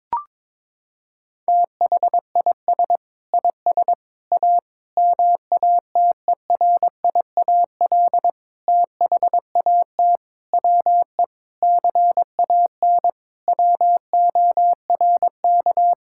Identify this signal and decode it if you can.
Morse code, heard as a keyed tone, about 22 wpm, 700 Hz. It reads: THIS IS A MATERIAL THAT WE CAN WORK
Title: Sentences from Top 1000 Words - Review Part 5 - 22wpm